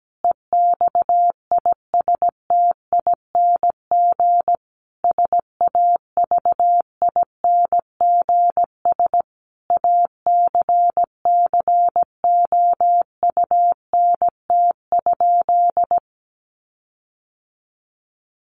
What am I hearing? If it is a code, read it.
Morse code, heard as a keyed tone, about 17 wpm, 700 Hz. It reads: EXISTING SAVINGS ACCOUNT?
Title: Sentences from Top 1500 Words 17wpm